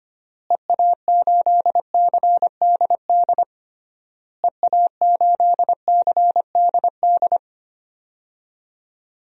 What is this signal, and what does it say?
Morse code, heard as a keyed tone, about 25 wpm, 700 Hz. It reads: EA8CDD EA8CDD
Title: DXCC-100 Call Signs - 6 Characters - Repeated 3x - 25wpm